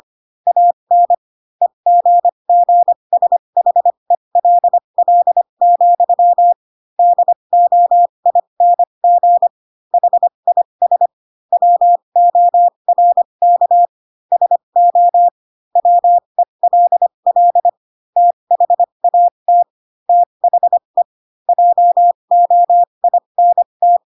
Morse code, heard as a keyed tone, about 25 wpm, spaced as for 22 wpm, 700 Hz. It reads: AN EGGSHELL, DOING HIS WORK SO WELL THAT THE JOINT